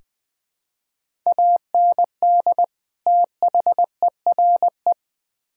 Morse code, heard as a keyed tone, about 21 wpm, 700 Hz. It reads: AND THERE
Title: Build A Phrase - 3-Word Phrases - 20wpm